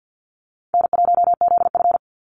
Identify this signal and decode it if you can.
Morse code, heard as a keyed tone, about 48 wpm, 700 Hz. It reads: N1ZF
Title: NA SOTA Activator Callsigns 50wpm